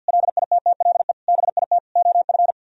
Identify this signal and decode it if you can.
Morse code, heard as a keyed tone, about 61 wpm, 700 Hz. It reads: LITTLE BIT OF
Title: Speed-Racing - Sets of 4 Words 50wpm